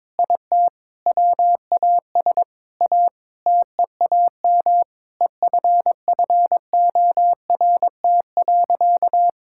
Morse code, heard as a keyed tone, about 21 wpm, 700 Hz. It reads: IT WAS A TEAM EFFORT.
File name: practice / 22wpm / Sentences with 8-Word Phrases - 22wpm